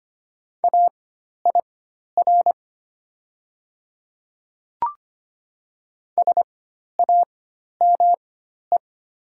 Morse code, heard as a keyed tone, about 25 wpm, spaced as for 12 wpm, 690 Hz. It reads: AIR SAME
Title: 4x Character Spacing - Top 200 Words - 25wpm